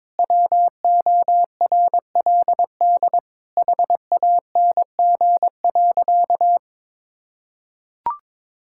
Morse code, heard as a keyed tone, about 22 wpm, 700 Hz. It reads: WORLD HANG.
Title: Sentences from Top 1600 Words 22wpm